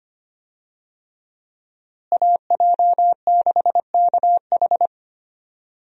Morse code, heard as a keyed tone, about 25 wpm, 700 Hz. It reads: AJ6KH
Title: USA Call Signs - 5 Characters - Repeated 3x - 25wpm